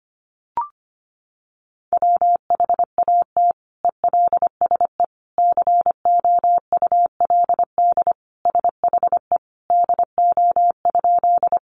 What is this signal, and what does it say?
Morse code, heard as a keyed tone, about 25 wpm, 700 Hz. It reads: WHAT ELSE COULD SHE DO?
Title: Sentences from Top 800 Words 25wpm